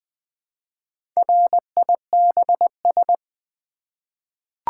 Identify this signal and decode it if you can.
Morse code, heard as a keyed tone, about 20 wpm, 700 Hz. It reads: RIBS
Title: Easy Words with Hints - 20wpm